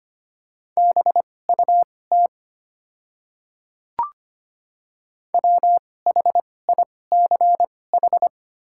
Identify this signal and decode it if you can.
Morse code, heard as a keyed tone, about 25 wpm, spaced as for 18 wpm, 700 Hz. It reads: BUT WHICH
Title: 2x Character Spacing - Top 300 Words - 25wpm